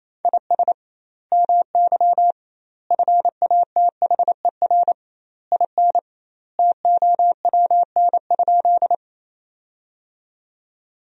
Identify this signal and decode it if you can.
Morse code, heard as a keyed tone, about 28 wpm, 700 Hz. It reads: IS MY FATHER IN TOWN?
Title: 2x Word-Spacing - Sentences from Top 500 Words - 28wpm